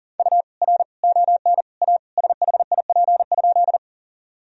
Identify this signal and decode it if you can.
Morse code, heard as a keyed tone, about 40 wpm, 700 Hz. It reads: U R ON A SHIP?
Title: Sentences from Top 400 Words and Abbreviations 40wpm